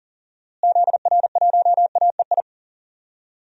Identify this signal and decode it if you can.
Morse code, heard as a keyed tone, about 40 wpm, 700 Hz. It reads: ZR1AEI
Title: DXCC-100 Call Signs - 6 Characters - Repeated 3x - 40wpm